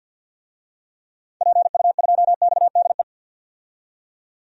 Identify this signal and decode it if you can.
Morse code, heard as a keyed tone, about 52 wpm, 700 Hz. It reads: PU2XDE
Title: DXCC-100 Call Signs - 6 Characters - 50wpm